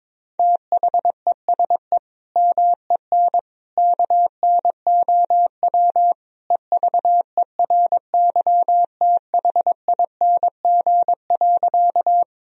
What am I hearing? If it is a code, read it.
Morse code, heard as a keyed tone, about 22 wpm, 700 Hz. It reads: THESE MEN KNOW EVERYTHING.
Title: Sentences from Top 1000 Words - Review Part 1 - 22wpm